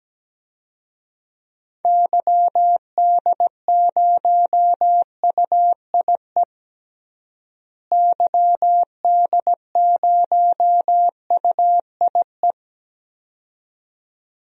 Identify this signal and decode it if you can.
Morse code, heard as a keyed tone, about 17 wpm, 700 Hz. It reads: YD0UIE YD0UIE
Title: DXCC-100 Call Signs - 6 Characters - Repeated 2x - 17wpm